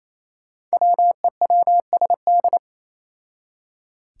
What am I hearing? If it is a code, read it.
Morse code, heard as a keyed tone, about 27 wpm, 700 Hz. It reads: WEWSD
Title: Five Challenging Letter-Number Combinations 28wpm